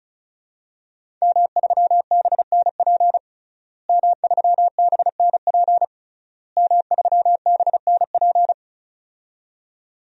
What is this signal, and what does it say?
Morse code, heard as a keyed tone, about 34 wpm, 700 Hz. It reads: M3BNP M3BNP M3BNP
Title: DXCC-100 Call Signs - 5 Characters - Repeated 3x - 35wpm